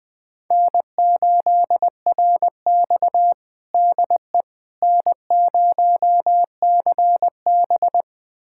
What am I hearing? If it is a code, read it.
Morse code, heard as a keyed tone, about 20 wpm, 700 Hz. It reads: N8RX DE N0CB
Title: USA Callsign DE Callsign - 4 Characters - 20wpm